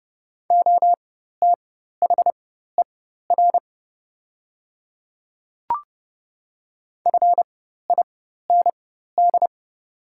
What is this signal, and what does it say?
Morse code, heard as a keyed tone, about 30 wpm, spaced as for 14 wpm, 700 Hz. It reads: OTHER FIND
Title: 4x Character Spacing - Top 100 Words - 30wpm